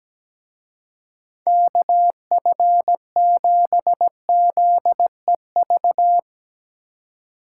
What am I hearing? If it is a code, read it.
Morse code, heard as a keyed tone, about 17 wpm, 700 Hz. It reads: KF7ZEV
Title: Speed Racing - USA 2x3 Call Signs 17wpm